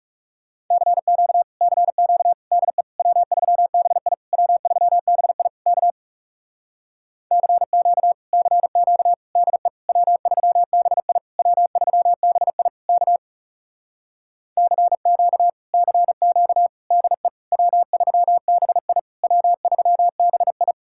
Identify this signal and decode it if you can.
Morse code, heard as a keyed tone, about 37 wpm, 700 Hz. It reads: CQ CQ DE W3BI W3BI K CQ CQ DE W3BI W3BI K CQ CQ DE W3BI W3BI
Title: Speed-Racing Common QSO Phrases - 30wpm